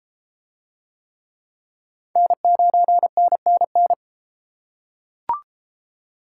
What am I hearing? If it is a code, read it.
Morse code, heard as a keyed tone, about 31 wpm, 700 Hz. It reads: N9NNN